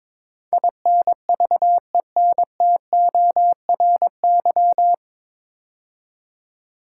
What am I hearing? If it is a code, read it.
Morse code, heard as a keyed tone, about 22 wpm, 700 Hz. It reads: INVENTORY
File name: 9 Letter Words - First 3 Letters Given - 22wpm